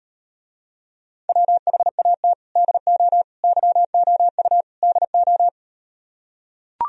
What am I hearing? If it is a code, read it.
Morse code, heard as a keyed tone, about 38 wpm, 700 Hz. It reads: WHAT DO YOU DO